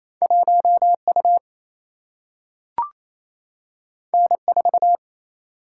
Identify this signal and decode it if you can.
Morse code, heard as a keyed tone, about 28 wpm, 700 Hz. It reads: 1U N4
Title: Single-Letter Single-Number 28wpm